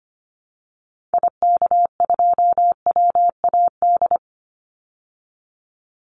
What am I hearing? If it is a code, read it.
Morse code, heard as a keyed tone, about 25 wpm, 700 Hz. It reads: IK2WAD
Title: Call Signs Repeated 2x - 25wpm